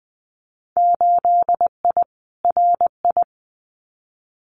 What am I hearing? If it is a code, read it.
Morse code, heard as a keyed tone, about 20 wpm, 700 Hz. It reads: 8I RI
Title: Winter Field Day Exchanges Encore 20wpm